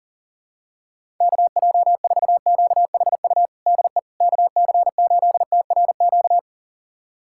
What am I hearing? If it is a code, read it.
Morse code, heard as a keyed tone, about 40 wpm, 700 Hz. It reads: KJ4QHU DE KC8TRQ
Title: USA Callsign DE Callsign 40wpm